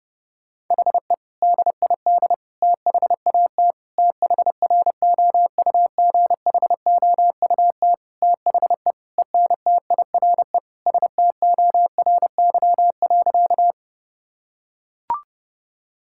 Morse code, heard as a keyed tone, about 30 wpm, 700 Hz. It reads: HE DID THAT THROUGHOUT THE ENTIRE STORY.
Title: Sentences from Top 1000 Words - Review Part 5 - 30wpm